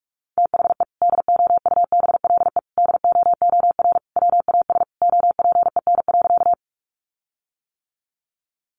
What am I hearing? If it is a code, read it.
Morse code, heard as a keyed tone, about 44 wpm, 700 Hz. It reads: THE DOUBLE DOOR WAS OPEN.
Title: Sentences from Top 1400 Words 45wpm